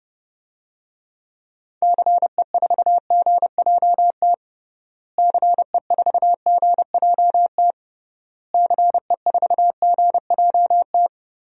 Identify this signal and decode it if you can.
Morse code, heard as a keyed tone, about 31 wpm, 700 Hz. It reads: CE4GJT CE4GJT CE4GJT
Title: DXCC-100 Call Signs - 6 Characters - Repeated 3x - 30wpm